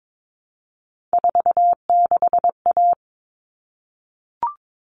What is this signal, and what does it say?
Morse code, heard as a keyed tone, about 22 wpm, 700 Hz. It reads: 46A
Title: All Cut Numbers - 3-Digit 22wpm